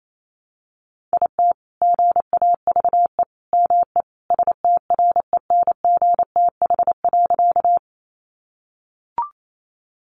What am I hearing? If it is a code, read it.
Morse code, heard as a keyed tone, about 28 wpm, 700 Hz. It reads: IT GAVE ME STRENGTH.
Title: Sentences from Top 1300 Words 28wpm